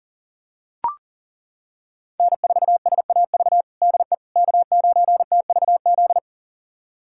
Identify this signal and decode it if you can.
Morse code, heard as a keyed tone, about 40 wpm, 700 Hz. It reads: N4SAV DE K9TVZ
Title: USA Callsign DE Callsign - 5 Characters - 40wpm